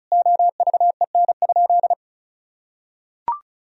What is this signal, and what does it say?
Morse code, heard as a keyed tone, about 34 wpm, 700 Hz. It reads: OVEN?
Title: Sentences from Top 1000 Words - Review Part 4 - 35wpm